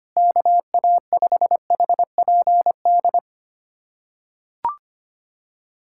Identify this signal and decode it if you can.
Morse code, heard as a keyed tone, about 25 wpm, 700 Hz. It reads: KA5HPD
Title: USA Callsign DE Callsign - 6 Characters - 25wpm